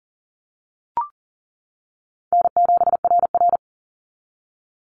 Morse code, heard as a keyed tone, about 40 wpm, 700 Hz. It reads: N7RR
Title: NA SOTA Activator Callsigns 40wpm